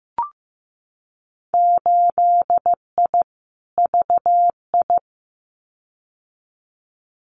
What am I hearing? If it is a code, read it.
Morse code, heard as a keyed tone, about 15 wpm, 700 Hz. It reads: 8I VI
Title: Winter Field Day Exchanges Encore 15wpm